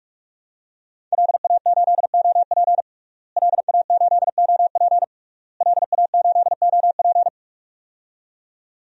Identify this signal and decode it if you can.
Morse code, heard as a keyed tone, about 46 wpm, 700 Hz. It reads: LA8OP LA8OP LA8OP
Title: DXCC-100 Call Signs - 5 Characters - Repeated 3x - 45wpm